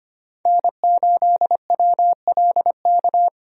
Morse code, heard as a keyed tone, about 25 wpm, 700 Hz. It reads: N8WLK
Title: USA Call Signs - 5 Characters - Repeated 3x - 25wpm